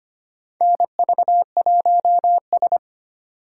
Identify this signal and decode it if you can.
Morse code, heard as a keyed tone, about 25 wpm, 700 Hz. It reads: NV1S
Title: USA Call Signs - 4 Characters - 3x Character Spacing - 25wpm